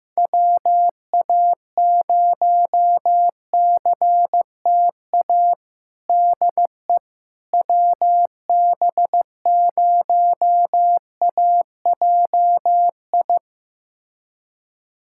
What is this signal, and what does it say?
Morse code, heard as a keyed tone, about 15 wpm, 700 Hz. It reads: WA0CTA DE WB0AJI
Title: USA Callsign DE Callsign - 6 Characters - 15wpm